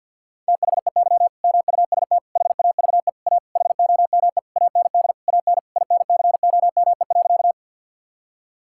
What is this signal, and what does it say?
Morse code, heard as a keyed tone, about 50 wpm, 700 Hz. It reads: THEY MUST HAVE A HOME AND AN INCOME.
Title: Sentences from Top 1000 Words - Review Part 5 - 50wpm